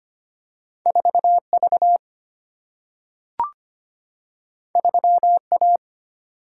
Morse code, heard as a keyed tone, about 25 wpm, 700 Hz. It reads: 4 V 3 A
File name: Two Letter-Number Combinations - 25wpm